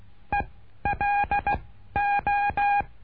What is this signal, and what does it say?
Morse code, heard as a keyed tone, about 16 wpm, 1650 Hz. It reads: E L O
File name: cursos / CURSO DE TELEGRAFIA / Telegrafia - Trilha 09